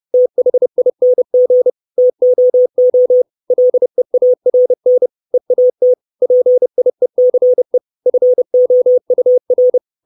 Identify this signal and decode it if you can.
Morse code, heard as a keyed tone, about 30 wpm, 500 Hz. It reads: THING TOO LEARN EAT PIECE FOUR
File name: 632 Words 30WPM